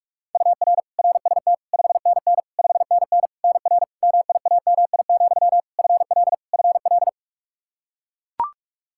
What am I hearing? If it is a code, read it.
Morse code, heard as a keyed tone, about 42 wpm, 700 Hz. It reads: UR RST 5NN 5NN NR MIAMI, FL FL
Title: K1USN SST QSO Practice 45wpm